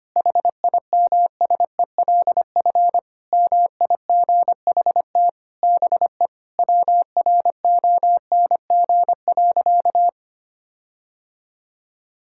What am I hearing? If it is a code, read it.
Morse code, heard as a keyed tone, about 25 wpm, 700 Hz. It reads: HIMSELF MIGHT BE WRONG.
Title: Sentences from Top 1600 Words 25wpm